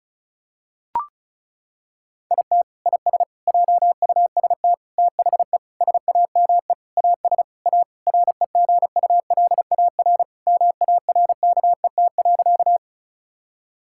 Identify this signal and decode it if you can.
Morse code, heard as a keyed tone, about 34 wpm, 700 Hz. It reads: IT IS JUST THE SAME AS A REGULAR MARKET.
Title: Sentences from Top 1000 Words - Review Part 6 - 35wpm